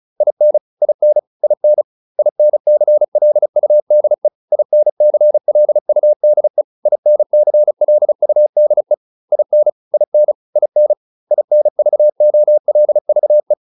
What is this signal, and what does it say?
Morse code, heard as a keyed tone, about 34 wpm, 600 Hz. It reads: IN IN IN INCLUDE INCLUDE INCLUDE IN IN IN INVOLVE
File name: IN7 35